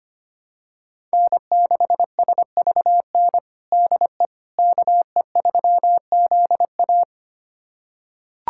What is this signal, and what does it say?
Morse code, heard as a keyed tone, about 25 wpm, 700 Hz. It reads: N6SVN DE KE3ZA
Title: USA Callsign DE Callsign - 5 Characters - 25wpm